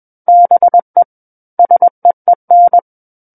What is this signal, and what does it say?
Morse code, heard as a keyed tone, about 21 wpm, 700 Hz. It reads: BE SEEN
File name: Book One / 12-Chapter 12 - 21 WPM